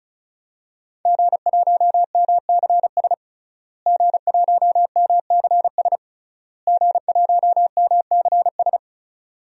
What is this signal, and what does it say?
Morse code, heard as a keyed tone, about 36 wpm, 700 Hz. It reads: G1MCS G1MCS G1MCS
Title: DXCC-100 Call Signs - 5 Characters - Repeated 3x - 35wpm